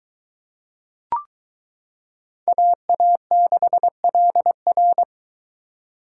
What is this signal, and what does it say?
Morse code, heard as a keyed tone, about 22 wpm, 700 Hz. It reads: AA6LR